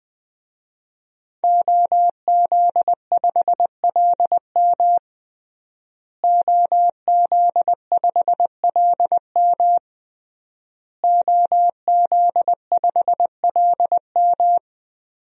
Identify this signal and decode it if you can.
Morse code, heard as a keyed tone, about 20 wpm, 700 Hz. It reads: OZ5LM OZ5LM OZ5LM
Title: DXCC-100 Call Signs - 5 Characters - Repeated 3x - 20wpm